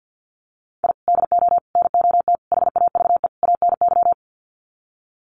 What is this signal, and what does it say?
Morse code, heard as a keyed tone, about 48 wpm, 700 Hz. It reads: I DO NOT HAVE ANY
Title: Sets of 5 Words 50wpm